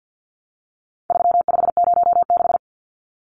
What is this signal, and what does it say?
Morse code, heard as a keyed tone, about 48 wpm, 700 Hz. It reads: 3506
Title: Numbers - 4-Digit 50wpm